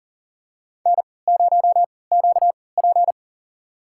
Morse code, heard as a keyed tone, about 37 wpm, spaced as for 22 wpm, 700 Hz. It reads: N0QP
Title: USA Call Signs - 3x Character Spacing - 40wpm